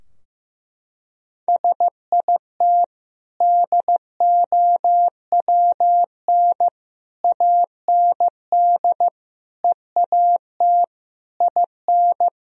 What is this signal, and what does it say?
Morse code, heard as a keyed tone, about 16 wpm, 700 Hz. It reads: SIT DOWN AND EAT IN
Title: Sentences from Top 1000 Words - Review Part 4 - 15wpm